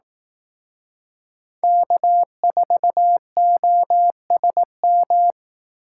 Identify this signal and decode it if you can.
Morse code, heard as a keyed tone, about 18 wpm, 700 Hz. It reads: K4OSM
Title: Speed Racing - USA 1x3 Call Signs 15wpm